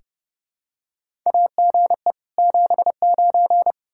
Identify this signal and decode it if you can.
Morse code, heard as a keyed tone, about 28 wpm, 700 Hz. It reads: AGE 79
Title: Practice QSO - Exchange 3 - More Info - 30wpm